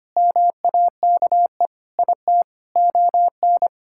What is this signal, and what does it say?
Morse code, heard as a keyed tone, about 25 wpm, 700 Hz. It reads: MAKE IT ON
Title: Sentences from Top 1000 Words - Review Part 6 - 25wpm